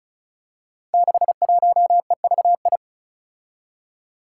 Morse code, heard as a keyed tone, about 36 wpm, 700 Hz. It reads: 61EVI
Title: Five Challenging Letter-Number Combinations 35wpm